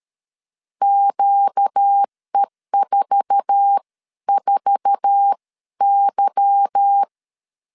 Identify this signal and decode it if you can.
Morse code, heard as a keyed tone, about 13 wpm, spaced as for 11 wpm, 800 Hz. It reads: QE44Y
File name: CW / DET / 1 9 1500C S B 12wpm 3819